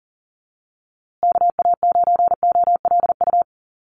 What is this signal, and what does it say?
Morse code, heard as a keyed tone, about 40 wpm, 700 Hz. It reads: KA9OLU
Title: USA Call Signs Repeated 2x 40wpm